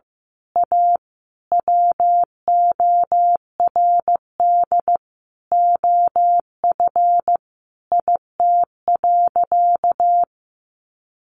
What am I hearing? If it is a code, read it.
Morse code, heard as a keyed tone, about 15 wpm, 700 Hz. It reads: A WORD OF IT.
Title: Sentences from Top 1200 Words 15wpm